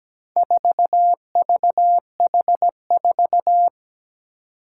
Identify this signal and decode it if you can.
Morse code, heard as a keyed tone, about 17 wpm, 700 Hz. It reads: 4VH4
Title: Four Challenging Letter-Number Combinations 17wpm